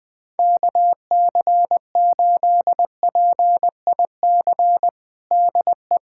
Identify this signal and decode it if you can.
Morse code, heard as a keyed tone, about 21 wpm, 700 Hz. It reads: KC8PIC DE
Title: USA Callsign DE Callsign - 6 Characters - 20wpm